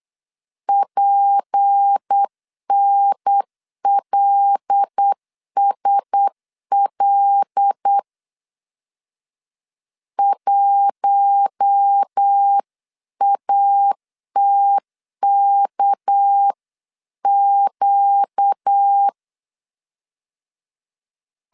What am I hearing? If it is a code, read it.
Morse code, heard as a keyed tone, about 8 wpm, 800 Hz. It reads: PNLSL 1ATKQ
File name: 8wpm